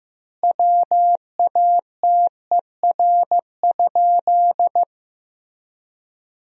Morse code, heard as a keyed tone, about 15 wpm, 700 Hz. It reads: WATER?
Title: 3x Word-Spacing - Sentences from Top 200 Words - 15wpm